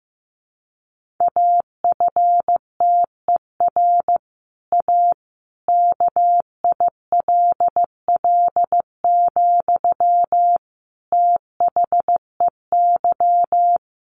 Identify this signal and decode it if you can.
Morse code, heard as a keyed tone, about 15 wpm, 700 Hz. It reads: AFTER A KILL, THEY
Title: Sentences from Top 1500 Words 15wpm